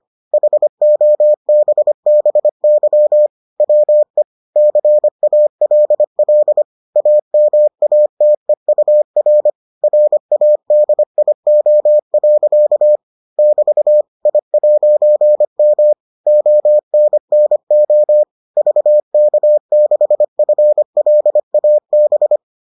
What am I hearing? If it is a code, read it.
Morse code, heard as a keyed tone, about 25 wpm, 600 Hz. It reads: HOBBY WE CALL AMATEUR RADIO. <BT> I'M ONNO VK6FLAB